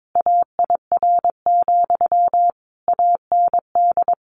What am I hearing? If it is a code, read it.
Morse code, heard as a keyed tone, about 22 wpm, 700 Hz. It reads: AIR, AND
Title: A Sound Of Thunder 22wpm